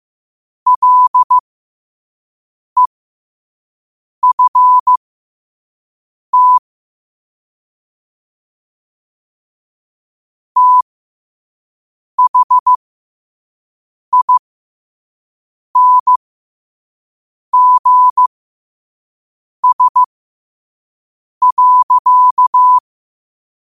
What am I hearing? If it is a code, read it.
Morse code, heard as a keyed tone, about 15 wpm, spaced as for 5 wpm, 1000 Hz. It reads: LEFT THINGS.
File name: qotdfiles / 5wpm